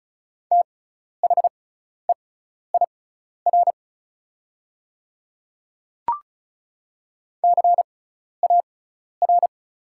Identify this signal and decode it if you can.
Morse code, heard as a keyed tone, about 34 wpm, spaced as for 12 wpm, 690 Hz. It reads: THEIR CAR
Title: 6x Character Spacing - Top 500 Words - 35wpm